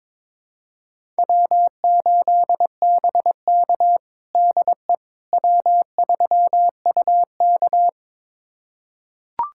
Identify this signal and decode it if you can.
Morse code, heard as a keyed tone, about 22 wpm, 700 Hz. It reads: W8BK DE W3UK
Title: USA Callsign DE Callsign - 4 Characters - 22wpm